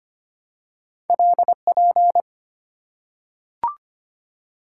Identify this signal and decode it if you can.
Morse code, heard as a keyed tone, about 25 wpm, 690 Hz. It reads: LP
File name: Two Letter-Number Combinations - 25wpm